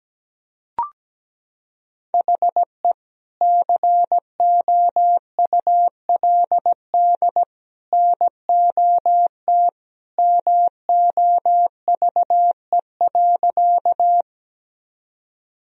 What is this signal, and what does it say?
Morse code, heard as a keyed tone, about 17 wpm, 700 Hz. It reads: HE COULD NOT MOVE.
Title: Sentences from Top 1000 Words - Review Part 1 - 17wpm